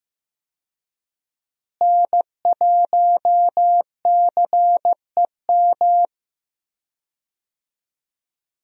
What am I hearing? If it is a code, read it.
Morse code, heard as a keyed tone, about 15 wpm, 700 Hz. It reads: N1CEM
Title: USA Call Signs - 5 Characters - Repeated 2x - 15wpm